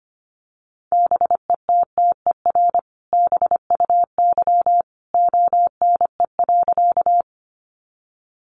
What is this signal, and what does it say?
Morse code, heard as a keyed tone, about 25 wpm, 700 Hz. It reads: BETTER BUY ONE.
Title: Sentences from Top 700 Words 25wpm